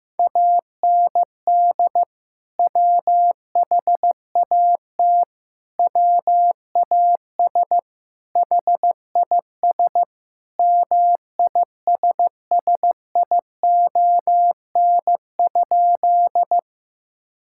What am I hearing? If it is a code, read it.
Morse code, heard as a keyed tone, about 15 wpm, 700 Hz. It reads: AND WHAT WAS HIS MISSION?
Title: Sentences from Top 1600 Words 15wpm